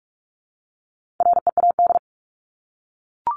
Morse code, heard as a keyed tone, about 44 wpm, 700 Hz. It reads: READ
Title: Top 300 Words Repeated 45wpm